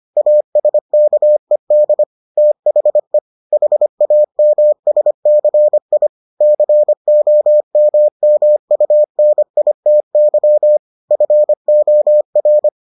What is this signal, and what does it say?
Morse code, heard as a keyed tone, about 25 wpm, 600 Hz. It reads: ASKED THE HAMSCI COMMUNITY FOR